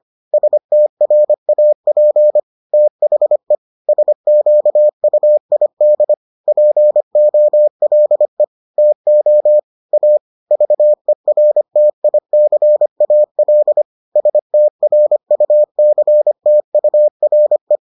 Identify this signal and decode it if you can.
Morse code, heard as a keyed tone, about 25 wpm, 600 Hz. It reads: STRAP THE SQUID POLE TO A VERTICAL STRUCTURE